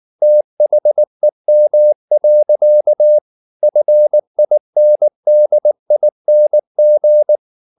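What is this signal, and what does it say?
Morse code, heard as a keyed tone, about 19 wpm, 600 Hz. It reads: THEM. FINDING
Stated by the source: Morse code at 19 wpm from CW med Gnister, reading Ken Follett's novel Never in English.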